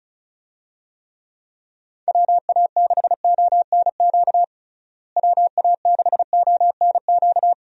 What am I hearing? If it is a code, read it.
Morse code, heard as a keyed tone, about 34 wpm, 700 Hz. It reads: WA6ONQ WA6ONQ
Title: USA Call Signs - 6 Characters - Repeated 2x - 35wpm